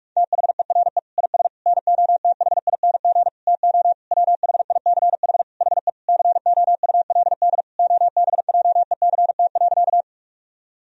Morse code, heard as a keyed tone, about 46 wpm, 700 Hz. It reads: THERE IS NOTHING TO WHICH HE COULD OBJECT.
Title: Sentences from Top 1000 Words - Review Part 3 - 45wpm